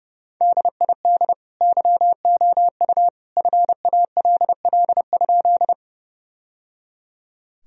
Morse code, heard as a keyed tone, about 30 wpm, 700 Hz. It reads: DID YOU FALL?
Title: Sentences from Top 1000 Words - Review Part 2 - 30wpm